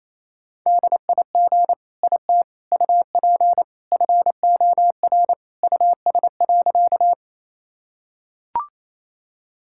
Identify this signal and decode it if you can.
Morse code, heard as a keyed tone, about 28 wpm, 700 Hz. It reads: DIG IT UP FOR US.
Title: Sentences from Top 1600 Words 28wpm